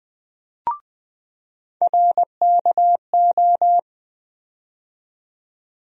Morse code, heard as a keyed tone, about 20 wpm, 700 Hz. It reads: RKO
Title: USA Call Sign Postfixes - 3 Letters 20wpm